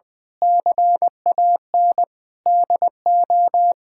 Morse code, heard as a keyed tone, about 20 wpm, 700 Hz. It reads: CAN DO
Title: Sentences with 9-Word Phrases - Spoken Beginning - 20wpm